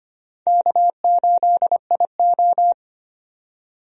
Morse code, heard as a keyed tone, about 25 wpm, 700 Hz. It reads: K8IO
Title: USA Call Signs - 4 Characters - 25wpm